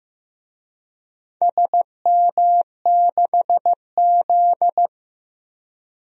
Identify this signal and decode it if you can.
Morse code, heard as a keyed tone, about 15 wpm, 700 Hz. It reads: SM6Z
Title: DXCC-100 Call Signs - 4 Characters - Repeated 2x - 15wpm